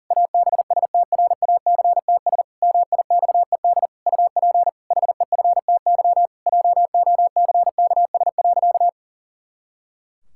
Morse code, heard as a keyed tone, about 40 wpm, 700 Hz. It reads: ABSTRACTS MIXED UP HEFTY JOCKS.
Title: Pangram Challenge v2 - 40wpm